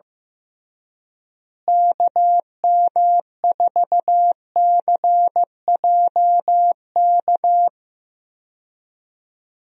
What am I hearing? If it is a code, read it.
Morse code, heard as a keyed tone, about 15 wpm, 700 Hz. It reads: KM4CJK
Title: USA Call Signs - 6 Characters - Repeated 3x - 15wpm